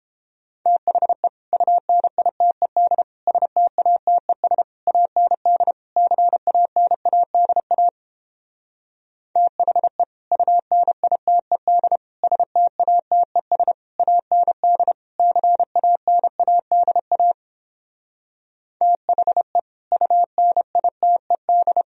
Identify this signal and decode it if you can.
Morse code, heard as a keyed tone, about 31 wpm, 700 Hz. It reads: THE UNITED STATES AND CANADA THE UNITED STATES AND CANADA THE UNITED